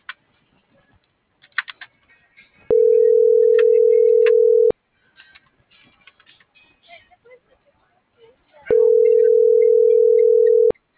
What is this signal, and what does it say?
Ringback tone (North America), 440 + 480 Hz.